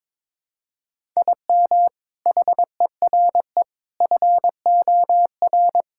Morse code, heard as a keyed tone, about 21 wpm, 700 Hz. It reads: IM HERE FOR
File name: practice / 22wpm / Sentences with 4-Word Phrases - 22wpm